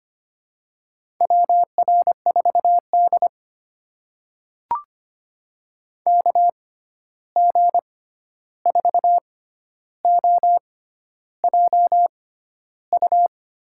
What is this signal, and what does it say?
Morse code, heard as a keyed tone, about 25 wpm, spaced as for 11 wpm, 700 Hz. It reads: WR4D KG4OJU
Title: USA Call Signs - 6x Character Spacing - 25wpm